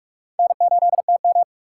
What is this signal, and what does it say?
Morse code, heard as a keyed tone, about 45 wpm, 690 Hz. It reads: N8TM
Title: USA Call Signs - 4 Characters - 2x Character Spacing - 45wpm